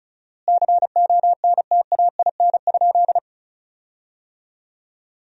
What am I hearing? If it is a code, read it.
Morse code, heard as a keyed tone, about 36 wpm, 700 Hz. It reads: CONTAIN?
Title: 4x Word-Spacing - Sentences from Top 500 Words - 35wpm